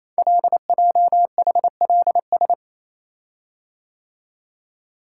Morse code, heard as a keyed tone, about 28 wpm, 700 Hz. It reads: LJHLS
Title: Code Groups - 5 Letters - 28wpm